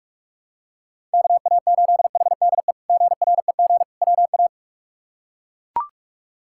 Morse code, heard as a keyed tone, about 41 wpm, 700 Hz. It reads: KA8HDE GREG WA
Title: K1USN SST QSO Practice - 45wpm